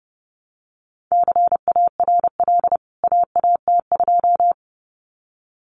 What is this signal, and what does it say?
Morse code, heard as a keyed tone, about 30 wpm, 700 Hz. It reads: CARL AAT2
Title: CWT Exchange 30wpm